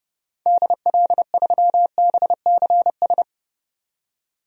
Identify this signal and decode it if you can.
Morse code, heard as a keyed tone, about 30 wpm, 700 Hz. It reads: DL3BCS
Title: DXCC-100 Call Signs - 6 Characters - 30wpm